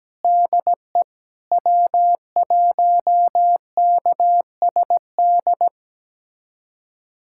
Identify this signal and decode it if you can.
Morse code, heard as a keyed tone, about 17 wpm, 700 Hz. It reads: DE W1KSD
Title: USA Callsign DE Callsign 17wpm